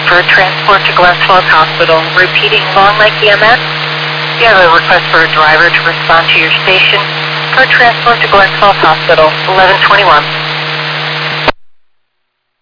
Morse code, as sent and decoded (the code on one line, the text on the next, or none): none